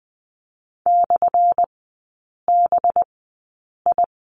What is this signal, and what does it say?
Morse code, heard as a keyed tone, about 20 wpm, spaced as for 8 wpm, 700 Hz. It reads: /BI
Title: 191 Lesson 31e - Letter v - Three Characters 20wpm